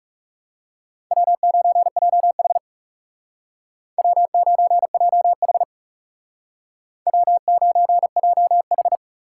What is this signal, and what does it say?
Morse code, heard as a keyed tone, about 40 wpm, 700 Hz. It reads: W9JH W9JH W9JH